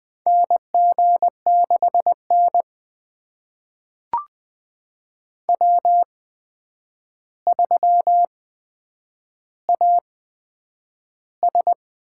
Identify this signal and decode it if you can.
Morse code, heard as a keyed tone, about 20 wpm, spaced as for 9 wpm, 700 Hz. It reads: NG6N W3AS
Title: USA Call Signs - 4 Characters - 8x Character Spacing - 20wpm